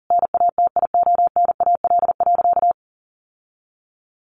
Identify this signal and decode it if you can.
Morse code, heard as a keyed tone, about 40 wpm, 700 Hz. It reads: NATIONAL.
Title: Sentences from Top 1200 Words 40wpm